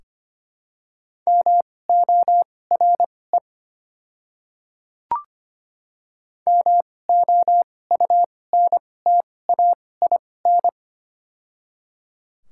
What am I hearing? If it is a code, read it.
Morse code, heard as a keyed tone, about 25 wpm, spaced as for 18 wpm, 700 Hz. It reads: MORE MOUNTAIN
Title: 2x Character Spacing - Top 300 Words - 25wpm